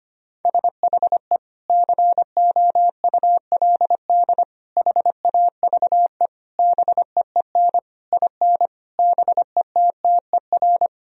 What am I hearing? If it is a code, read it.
Morse code, heard as a keyed tone, about 25 wpm, 700 Hz. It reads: SHE COULD HAVE BEEN IN BETTER
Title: Sentences from Top 1000 Words - Review Part 1 - 25wpm